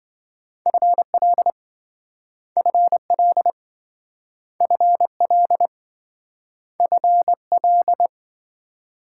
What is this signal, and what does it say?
Morse code, heard as a keyed tone, about 26 wpm, 700 Hz. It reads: FL FL FL FL